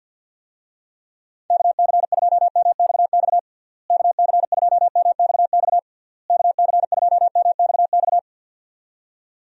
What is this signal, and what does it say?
Morse code, heard as a keyed tone, about 51 wpm, 700 Hz. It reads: KC2MXX KC2MXX KC2MXX
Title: USA Call Signs - 6 Characters - Repeated 3x - 50wpm